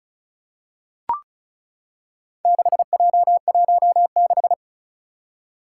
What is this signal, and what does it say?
Morse code, heard as a keyed tone, about 36 wpm, 690 Hz. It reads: 6J16
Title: Four Challenging Letter-Number Combinations 35wpm